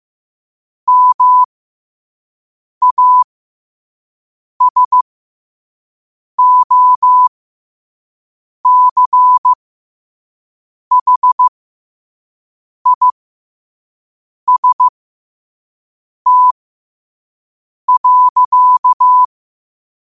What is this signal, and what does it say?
Morse code, heard as a keyed tone, about 15 wpm, spaced as for 5 wpm, 1000 Hz. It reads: MASOCHIST.
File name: qotdfiles / 5wpm